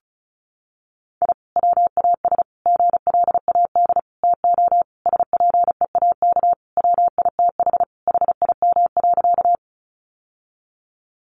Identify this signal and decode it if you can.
Morse code, heard as a keyed tone, about 34 wpm, 700 Hz. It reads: I WAS GLAD TO SPEAK WITH HIM.
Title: Sentences from Top 1100 Words 35wpm